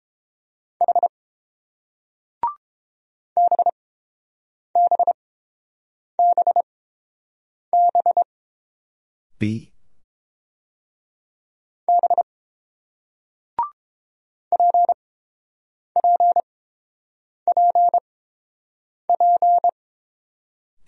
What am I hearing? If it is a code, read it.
Morse code, heard as a keyed tone, about 30 wpm, spaced as for 7 wpm, 700 Hz. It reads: H BBBB B PPPP